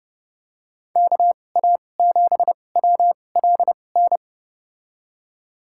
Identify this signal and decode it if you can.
Morse code, heard as a keyed tone, about 30 wpm, spaced as for 22 wpm, 700 Hz. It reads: KA7WLN
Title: USA Call Signs - 6 Characters - 2x Character Spacing - 30wpm